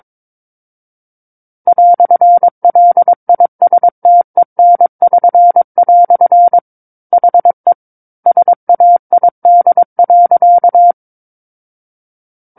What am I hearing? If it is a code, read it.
Morse code, heard as a keyed tone, about 22 wpm, 700 Hz. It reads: "LISTEN<SN>" HE SAID.
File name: Book One / 13-Chapter 13 - 22 WPM